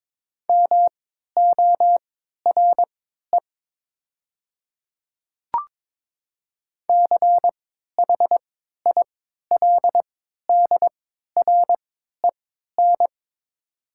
Morse code, heard as a keyed tone, about 22 wpm, spaced as for 12 wpm, 700 Hz. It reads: MORE CHILDREN
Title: 3x Character Spacing - Top 300 Words - 22wpm